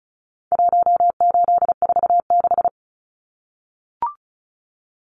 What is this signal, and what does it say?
Morse code, heard as a keyed tone, about 34 wpm, 700 Hz. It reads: 1846
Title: Numbers - Contest Style - 4-Digit 35wpm